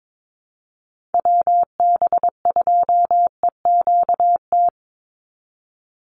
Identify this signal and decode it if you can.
Morse code, heard as a keyed tone, about 22 wpm, 700 Hz. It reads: WB2EQT
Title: USA Call Signs 22wpm